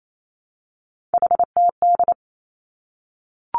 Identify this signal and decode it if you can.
Morse code, heard as a keyed tone, about 28 wpm, 700 Hz. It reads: HTD
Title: Three Challenging Letter-Number Combinations 28wpm